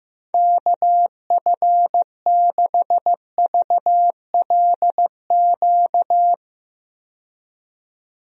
Morse code, heard as a keyed tone, about 15 wpm, 700 Hz. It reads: KF6VLQ
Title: USA Call Signs - 6 Characters - 15wpm